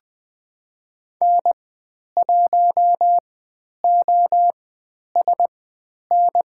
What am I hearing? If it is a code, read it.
Morse code, heard as a keyed tone, about 20 wpm, spaced as for 10 wpm, 700 Hz. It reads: N1OSN
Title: 109 Lesson 18g - Letter m - Call Signs 20wpm